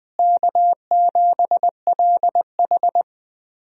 Morse code, heard as a keyed tone, about 21 wpm, 700 Hz. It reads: K7LH
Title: USA Call Signs - 4 Characters - Repeated 2x - 20wpm